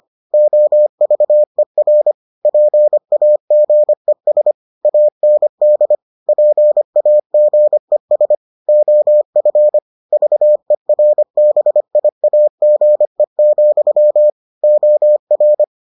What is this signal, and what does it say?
Morse code, heard as a keyed tone, about 25 wpm, 600 Hz. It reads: OVER PAGES AND PAGES OF VERBIAGE, OR